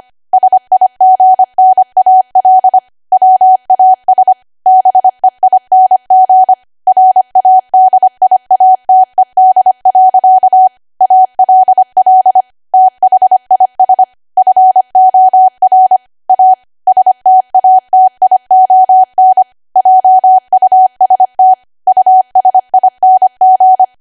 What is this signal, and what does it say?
Morse code, heard as a keyed tone, about 25 wpm, 750 Hz. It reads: SIGNAL WAS BEING RADIATED. ALL THIS FOR A STATION JUST USING
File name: Archive / 25 WPM / 25WPM